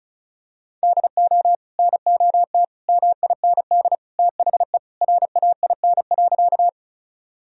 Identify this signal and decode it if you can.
Morse code, heard as a keyed tone, about 36 wpm, 700 Hz. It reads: DO NOT MIND THE RAIN.
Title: Sentences from Top 1000 Words - Review Part 1 - 35wpm